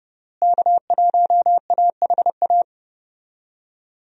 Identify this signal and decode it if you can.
Morse code, heard as a keyed tone, about 30 wpm, 700 Hz. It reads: K1AHA
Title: USA Call Signs - 5 Characters - Repeated 3x - 30wpm